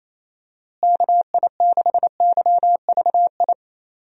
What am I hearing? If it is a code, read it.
Morse code, heard as a keyed tone, about 28 wpm, 700 Hz. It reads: KI6YVI
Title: USA Call Signs - 6 Characters - Repeated 2x - 28wpm